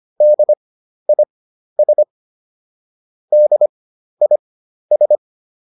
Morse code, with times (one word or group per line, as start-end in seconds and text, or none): 0.2-2.0 DIS
3.3-5.2 DIS